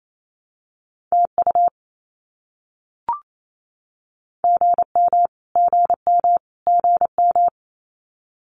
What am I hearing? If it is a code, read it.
Morse code, heard as a keyed tone, about 28 wpm, 700 Hz. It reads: TU GM GM GM
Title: Abbreviations Repeated 3x - 28wpm